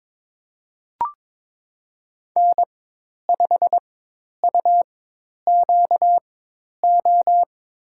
Morse code, heard as a keyed tone, about 22 wpm, spaced as for 10 wpm, 700 Hz. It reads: N5UQO
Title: USA Call Signs - 5 Characters - 4x Character Spacing - 22wpm